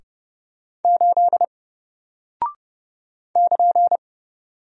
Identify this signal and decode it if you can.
Morse code, heard as a keyed tone, about 30 wpm, 700 Hz. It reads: 8 <KN>
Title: Single Character (Letter:Number:Punctuation:Prosign) Alt 30wpm